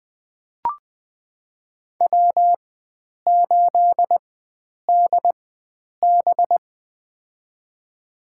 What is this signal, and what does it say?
Morse code, heard as a keyed tone, about 20 wpm, spaced as for 9 wpm, 700 Hz. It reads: W8DB
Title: USA Call Signs - 4x Character Spacing - 20wpm